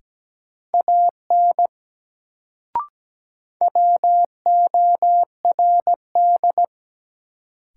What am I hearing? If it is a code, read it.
Morse code, heard as a keyed tone, about 17 wpm, 700 Hz. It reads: AN WORD
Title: Sentences from Top 300 Words - New Word Drill - 17wpm